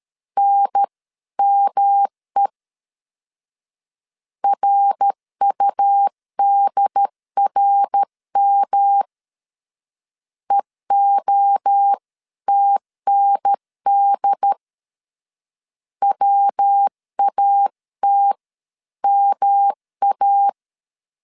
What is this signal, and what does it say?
Morse code, heard as a keyed tone, about 13 wpm, 800 Hz. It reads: NME RUDRM EOTND WAT MA
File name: CW / DET / 1 3 1500C S A 12wpm 2905